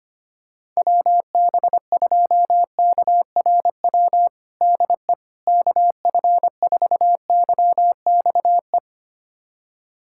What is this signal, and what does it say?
Morse code, heard as a keyed tone, about 25 wpm, 700 Hz. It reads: WB2KRW DE KF4YXE
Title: USA Callsign DE Callsign - 6 Characters - 25wpm